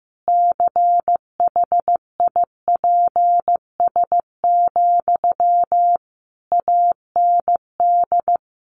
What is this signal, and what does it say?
Morse code, heard as a keyed tone, about 15 wpm, 700 Hz. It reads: CHIPS, AND
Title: Easy Phrases 15wpm